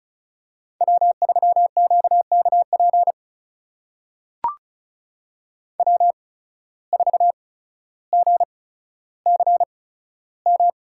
Morse code, heard as a keyed tone, about 34 wpm, spaced as for 15 wpm, 700 Hz. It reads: W3QKP W4GCM
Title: USA Call Signs - 8x Character Spacing - 35wpm